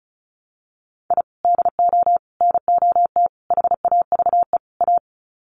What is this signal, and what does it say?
Morse code, heard as a keyed tone, about 34 wpm, 700 Hz. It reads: I DO NOT HAVE A
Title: Sets of 5 Words 35wpm